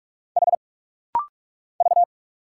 Morse code, heard as a keyed tone, about 45 wpm, 700 Hz. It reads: H V
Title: Single Letter-Number - Rapid-Fire 45wpm